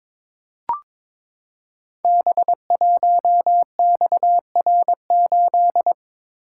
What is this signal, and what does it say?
Morse code, heard as a keyed tone, about 22 wpm, 700 Hz. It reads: B1XR8
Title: Code Groups - 5 Letter:Numbers - 22wpm